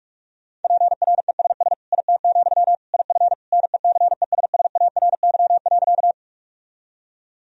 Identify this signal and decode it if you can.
Morse code, heard as a keyed tone, about 46 wpm, 700 Hz. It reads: PRESS IT, IF NECESSARY.
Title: Sentences from Top 1000 Words - Review Part 1 - 45wpm